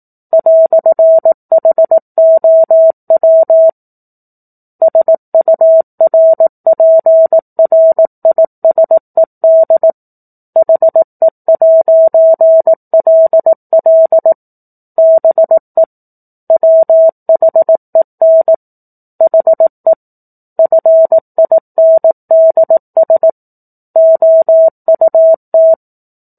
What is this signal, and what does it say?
Morse code, heard as a keyed tone, about 18 wpm, 650 Hz. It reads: "HOW SURPRISED HE'LL BE WHEN HE FINDS OUT
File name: 04-Chapter-04 18 WPM